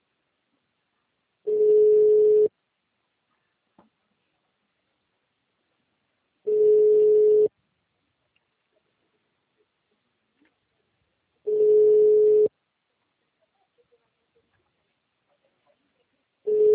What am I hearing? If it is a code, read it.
Ringback tone (Europe, 425 Hz).
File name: out- -2012- - - .46461